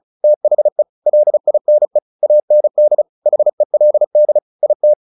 Morse code, heard as a keyed tone, about 34 wpm, 600 Hz. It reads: THE LINE AND HELD IT
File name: the old man and the sea / the old man and the sea 1 35wpm